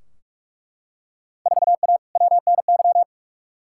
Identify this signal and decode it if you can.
Morse code, heard as a keyed tone, about 41 wpm, 700 Hz. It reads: 4A WNY
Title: Field Day Exchanges Encore - 45wpm